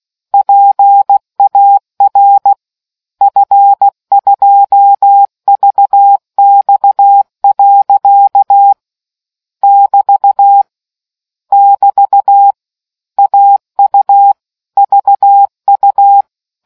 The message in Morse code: PAR F2VX. <BT> <BT> AU VU